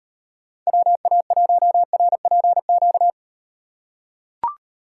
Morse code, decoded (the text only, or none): WA1RPQ